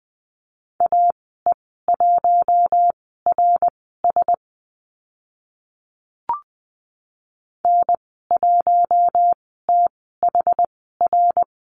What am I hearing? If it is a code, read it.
Morse code, heard as a keyed tone, about 20 wpm, spaced as for 15 wpm, 700 Hz. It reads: AE1RS N1THR
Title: 069 Lesson 12g - Letter d - Call Signs 20wpm